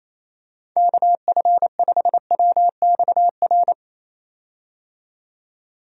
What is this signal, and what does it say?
Morse code, heard as a keyed tone, about 28 wpm, 700 Hz. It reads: KF5WXR
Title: USA Call Signs - 6 Characters - 28wpm